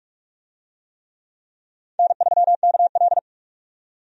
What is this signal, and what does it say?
Morse code, heard as a keyed tone, about 47 wpm, 700 Hz. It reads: N3KL
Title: USA Call Signs - 4 Characters - 45wpm